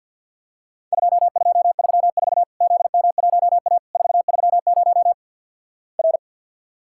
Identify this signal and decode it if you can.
Morse code, heard as a keyed tone, about 53 wpm, 700 Hz. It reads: 2234 ZM1A 430 R
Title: Worked All Europe DX Contest - QTC Exchange with Repeats - 50wpm